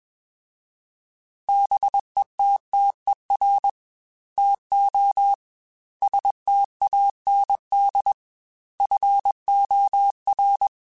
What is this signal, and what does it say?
Morse code, heard as a keyed tone, about 21 wpm, 800 Hz. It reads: BETTER TO STAND FOR
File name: Quotes / Quotes-mp3s / Temper @20